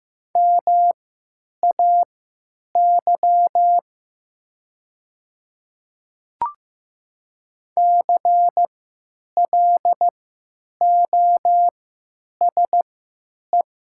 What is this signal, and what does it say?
Morse code, heard as a keyed tone, about 15 wpm, spaced as for 9 wpm, 700 Hz. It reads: MAY CLOSE
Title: 3x Character Spacing - Top 300 Words - 15wpm